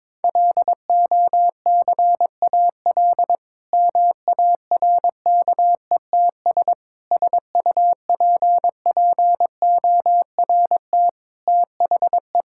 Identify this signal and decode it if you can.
Morse code, heard as a keyed tone, about 21 wpm, 700 Hz. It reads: LOCAL MARKETS SUPPORT THE
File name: Difficult Sentences with Hints - 22wpm